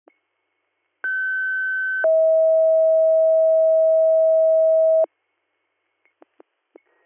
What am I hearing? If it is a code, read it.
Two-tone page, heard as a one-second tone, then a three-second tone, about 1550 then 640 Hz. It is heard from 1.0 to 5.0 s.